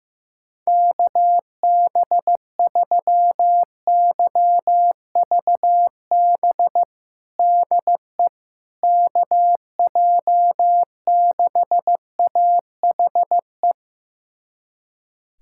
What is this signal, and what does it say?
Morse code, heard as a keyed tone, about 15 wpm, 700 Hz. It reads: KB3YVB DE KJ6AHE
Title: USA Callsign DE Callsign - 6 Characters - 15wpm